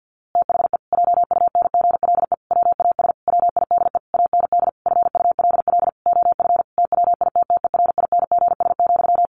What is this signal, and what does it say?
Morse code, heard as a keyed tone, about 50 wpm, 700 Hz. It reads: THE JUNGLE WAS WIDE AND FULL OF TWITTERINGS,
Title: A Sound Of Thunder 50wpm